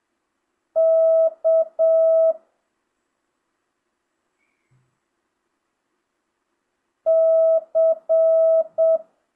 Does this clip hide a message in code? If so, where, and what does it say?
Yes: Morse code from 0.8 to 9.0 s, heard as a keyed tone, about 7 wpm, 650 Hz. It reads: K C